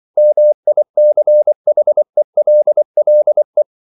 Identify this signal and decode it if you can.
Morse code, heard as a keyed tone, about 24 wpm, 600 Hz. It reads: MICHELLE
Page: Ferie 24 wpm